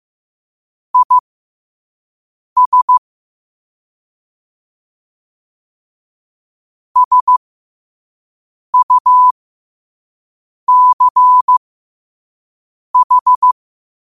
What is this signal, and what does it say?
Morse code, heard as a keyed tone, about 15 wpm, spaced as for 5 wpm, 1000 Hz. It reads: IS SUCH
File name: qotdfiles / 5wpm